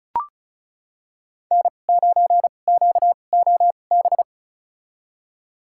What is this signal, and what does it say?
Morse code, heard as a keyed tone, about 36 wpm, spaced as for 25 wpm, 700 Hz. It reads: N9QOB
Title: USA Call Signs - 2x Character Spacing - 35wpm